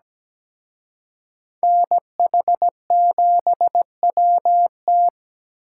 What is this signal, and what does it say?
Morse code, heard as a keyed tone, about 20 wpm, 700 Hz. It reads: NH7WT
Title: Speed Racing - USA 2x2 Call Signs 17wpm